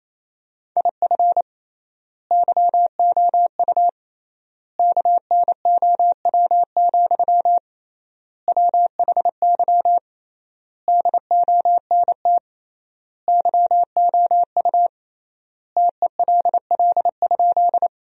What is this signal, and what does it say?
Morse code, heard as a keyed tone, about 27 wpm, 700 Hz. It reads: IF YOU KNOW, WHY DONT YOU TELL?
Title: 3x Word-Spacing - Sentences from Top 300 Words - 28wpm